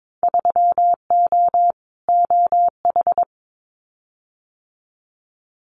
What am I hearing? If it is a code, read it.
Morse code, heard as a keyed tone, about 22 wpm, 700 Hz. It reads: 3O OH
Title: Winter Field Day Exchanges 22wpm